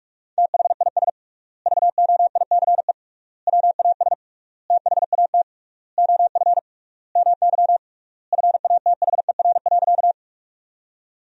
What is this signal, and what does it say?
Morse code, heard as a keyed tone, about 46 wpm, 700 Hz. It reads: THIS VOICE WAS THAT OF MY FATHER.
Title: 3x Word-Spacing - Sentences from Top 500 Words - 45wpm